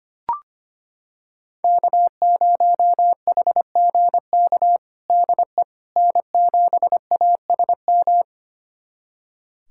Morse code, heard as a keyed tone, about 25 wpm, 700 Hz. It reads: K0HGK DE N7ASM
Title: USA Callsign DE Callsign - 5 Characters - 25wpm